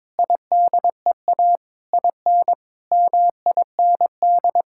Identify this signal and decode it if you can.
Morse code, heard as a keyed tone, about 22 wpm, 700 Hz. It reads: IDEA IN MIND
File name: Sets of 9 Words - 22wpm